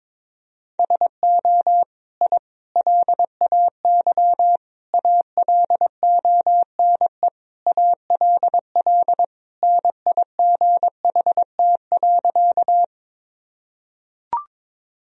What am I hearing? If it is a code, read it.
Morse code, heard as a keyed tone, about 21 wpm, 700 Hz. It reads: SO I LAY ALONE ALL NIGHT.
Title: Sentences from Top 1000 Words - Review Part 2 - 22wpm